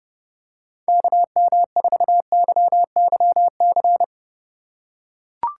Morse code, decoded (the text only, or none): KM4YYC